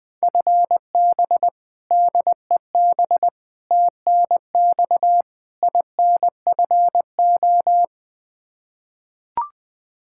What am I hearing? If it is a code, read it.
Morse code, heard as a keyed tone, about 20 wpm, 700 Hz. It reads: FB DEB TNX INFO
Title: K1USN SST QSO Practice 20wpm